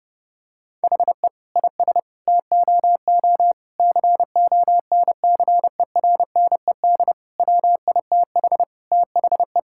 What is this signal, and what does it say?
Morse code, heard as a keyed tone, about 28 wpm, 700 Hz. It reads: HE IS TOO CONCERNED WITH THE
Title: Sentences from Top 1000 Words - Review Part 6 - 30wpm